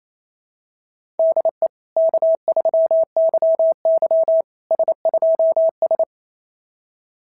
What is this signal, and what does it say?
Morse code, heard as a keyed tone, about 27 wpm, 690 Hz. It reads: DE K3YY S2S
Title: SOTA QSO Practice 28wpm